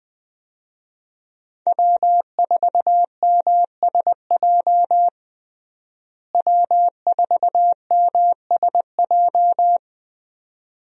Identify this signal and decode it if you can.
Morse code, heard as a keyed tone, about 20 wpm, 700 Hz. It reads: W4MSJ W4MSJ
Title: USA Call Signs - 5 Characters - Repeated 2x - 20wpm